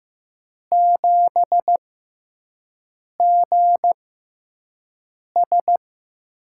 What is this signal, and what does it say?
Morse code, heard as a keyed tone, about 15 wpm, spaced as for 5 wpm, 700 Hz. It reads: 7GS
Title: USA Call Signs - 5 Characters - 6x Character Spacing - 15wpm